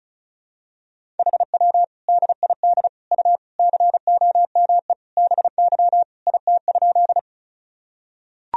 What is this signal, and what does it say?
Morse code, heard as a keyed tone, about 36 wpm, 700 Hz. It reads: HW DID U COME BY IT?
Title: Sentences from Top 100 Words and Abbreviations 35wpm